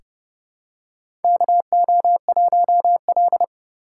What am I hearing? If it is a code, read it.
Morse code, heard as a keyed tone, about 30 wpm, 700 Hz. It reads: KO1L
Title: USA Call Signs - 4 Characters - 2x Character Spacing - 30wpm